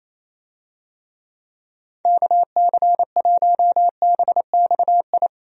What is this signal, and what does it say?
Morse code, heard as a keyed tone, about 27 wpm, 700 Hz. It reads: KC1BXI
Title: USA Call Signs - 6 Characters - Repeated 2x - 28wpm